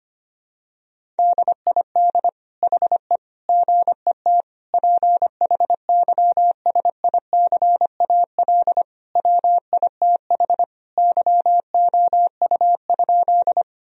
Morse code, heard as a keyed tone, about 25 wpm, 700 Hz. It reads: DID HE GET PHYSICAL WITH YOU?
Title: Sentences from Top 1000 Words - Review Part 4 - 25wpm